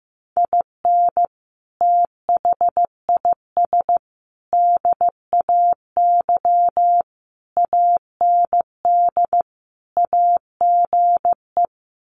Morse code, heard as a keyed tone, about 15 wpm, 700 Hz. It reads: IN THIS DAY AND AGE
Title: Sets of 5 Words Encore 15wpm